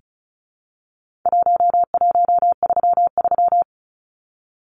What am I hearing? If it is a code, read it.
Morse code, heard as a keyed tone, about 34 wpm, 700 Hz. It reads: 1133
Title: Numbers - Contest Style - 4-Digit 35wpm